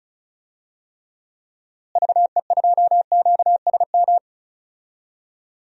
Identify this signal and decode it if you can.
Morse code, heard as a keyed tone, about 34 wpm, 700 Hz. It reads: VE2QSM
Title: DXCC-100 Call Signs - 6 Characters - Repeated 2x - 35wpm